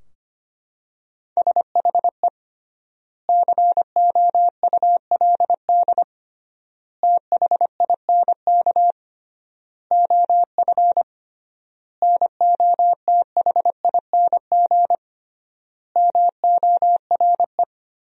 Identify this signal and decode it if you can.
Morse code, heard as a keyed tone, about 25 wpm, 700 Hz. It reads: SHE COULD THINK OF NOTHING MORE
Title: 3x Word-Spacing - Sentences from Top 500 Words - 25wpm